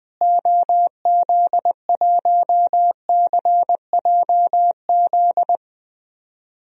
Morse code, heard as a keyed tone, about 21 wpm, 700 Hz. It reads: OZ1CJZ
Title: DXCC-100 Call Signs - 6 Characters - Repeated 3x - 20wpm